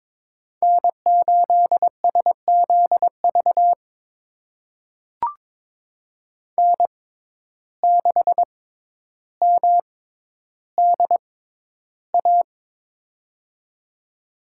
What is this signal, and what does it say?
Morse code, heard as a keyed tone, about 22 wpm, spaced as for 11 wpm, 700 Hz. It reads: N8SZV N6MDA
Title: USA Call Signs - 6x Character Spacing - 22wpm